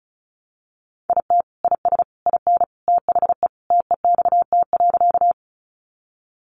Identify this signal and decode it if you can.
Morse code, heard as a keyed tone, about 34 wpm, 700 Hz. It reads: IT IS IN THE TEXT.
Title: Sentences from Top 1200 Words 35wpm